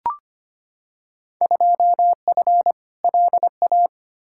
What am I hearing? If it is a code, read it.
Morse code, heard as a keyed tone, about 25 wpm, 700 Hz. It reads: 2F LA
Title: Field Day Exchanges - 25wpm